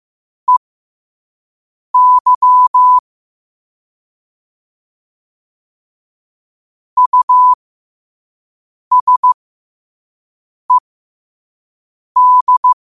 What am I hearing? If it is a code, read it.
Morse code, heard as a keyed tone, about 15 wpm, spaced as for 5 wpm, 1000 Hz. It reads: EY USED
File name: qotdfiles / 5wpm